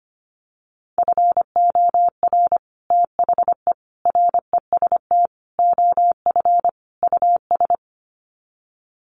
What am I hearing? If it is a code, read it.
Morse code, heard as a keyed tone, about 25 wpm, 700 Hz. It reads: FOR THE REST OF US
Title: Sets of 5 Words Encore 25wpm